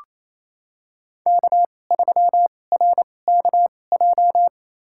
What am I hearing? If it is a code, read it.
Morse code, heard as a keyed tone, about 29 wpm, spaced as for 21 wpm, 700 Hz. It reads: K3RKJ
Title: USA Call Signs - 5 Characters - 2x Character Spacing - 28wpm